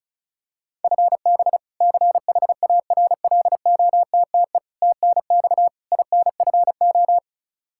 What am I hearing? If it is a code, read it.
Morse code, heard as a keyed tone, about 36 wpm, 700 Hz. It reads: FB CHARLOTTE TNX INFO
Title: Practice QSO - Exchange 3 - More Info - 35wpm